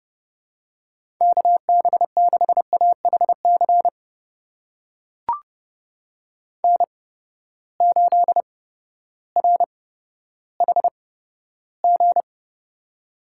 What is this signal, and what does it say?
Morse code, heard as a keyed tone, about 30 wpm, spaced as for 14 wpm, 700 Hz. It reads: KB6AHC N8RHG
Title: USA Call Signs - 8x Character Spacing - 30wpm